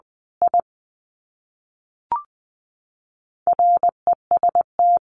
Morse code, heard as a keyed tone, about 20 wpm, 700 Hz. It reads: I REST
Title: Sets of 2 Words 20wpm